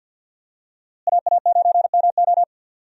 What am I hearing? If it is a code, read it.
Morse code, heard as a keyed tone, about 48 wpm, 700 Hz. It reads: AA9MO
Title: USA Call Signs - 5 Characters - Repeated 2x - 50wpm